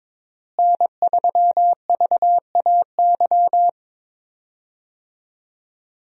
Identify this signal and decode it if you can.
Morse code, heard as a keyed tone, about 22 wpm, 700 Hz. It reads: N3VAY
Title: USA Call Signs - 5 Characters - Repeated 2x - 22wpm